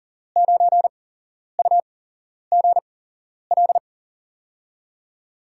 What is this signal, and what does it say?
Morse code, heard as a keyed tone, about 40 wpm, spaced as for 11 wpm, 700 Hz. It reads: 9UGL
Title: USA Call Signs - 6 Characters - 8x Character Spacing - 40wpm